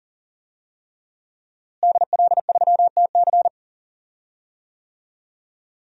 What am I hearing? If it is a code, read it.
Morse code, heard as a keyed tone, about 40 wpm, 700 Hz. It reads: DL3TC
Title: DXCC-100 Call Signs - 5 Characters - Repeated 3x - 40wpm